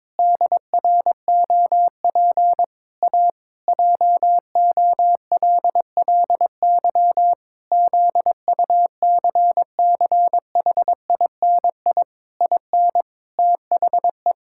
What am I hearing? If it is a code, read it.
Morse code, heard as a keyed tone, about 22 wpm, 700 Hz. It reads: DROP A JOLLY ZUCCHINI IN THE
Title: Pangram Challenge v2 - 22wpm